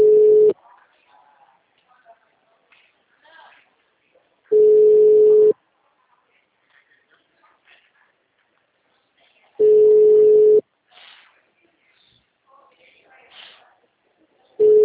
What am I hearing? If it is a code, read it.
Ringback tone (Europe, 425 Hz), with speech.